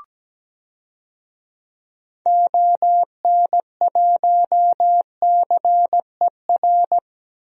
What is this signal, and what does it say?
Morse code, heard as a keyed tone, about 17 wpm, 700 Hz. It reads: ON1CER
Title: DXCC-100 Call Signs - 6 Characters - 17wpm